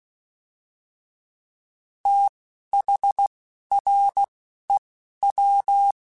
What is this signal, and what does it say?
Morse code, heard as a keyed tone, about 16 wpm, spaced as for 11 wpm, 800 Hz. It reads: THREW
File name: TH / TH mp3s / Fast TH @10